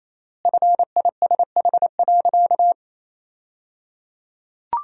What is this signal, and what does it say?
Morse code, heard as a keyed tone, about 28 wpm, 700 Hz. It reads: FISH.
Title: 3x Word-Spacing - Sentences from Top 300 Words - 28wpm